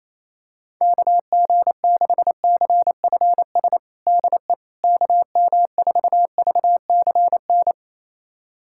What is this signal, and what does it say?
Morse code, heard as a keyed tone, about 27 wpm, 700 Hz. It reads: KG6CFS DE KM4VCN
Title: USA Callsign DE Callsign - 6 Characters - 28wpm